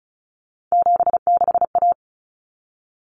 Morse code, heard as a keyed tone, about 34 wpm, 700 Hz. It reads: 76A
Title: Common Cut Numbers - 3-Digit 35wpm